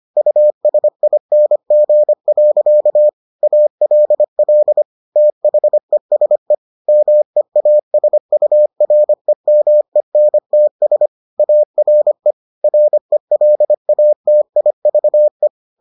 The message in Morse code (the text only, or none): USING. ALL THESE MEASUREMENTS ARE RELATIVE